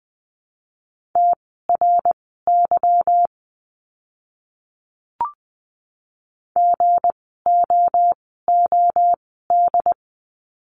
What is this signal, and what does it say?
Morse code, heard as a keyed tone, about 20 wpm, spaced as for 14 wpm, 700 Hz. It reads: TRY GOOD
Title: 160 Lesson 26f - Letter g - Words 20wpm